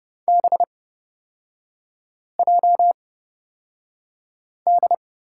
Morse code, heard as a keyed tone, about 30 wpm, spaced as for 5 wpm, 700 Hz. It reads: BJD
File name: Lesson 36e - BT - Three Characters 30wpm